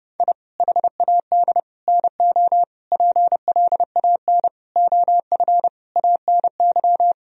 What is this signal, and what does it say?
Morse code, heard as a keyed tone, about 30 wpm, 700 Hz. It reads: I HAD NO PLAN OF ANY
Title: Sentences from Top 1000 Words - Review Part 2 - 30wpm